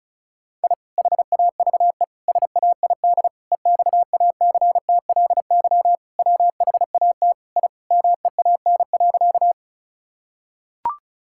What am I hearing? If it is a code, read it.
Morse code, heard as a keyed tone, about 36 wpm, 700 Hz. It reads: I HAVE SAID EXACTLY WHAT I MEAN.
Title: Sentences from Top 1000 Words - Review Part 6 - 35wpm